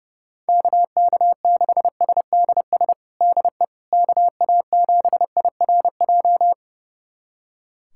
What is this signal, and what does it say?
Morse code, heard as a keyed tone, about 30 wpm, 700 Hz. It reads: KK6SDS DE KA7IRJ
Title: USA Callsign DE Callsign - 6 Characters - 30wpm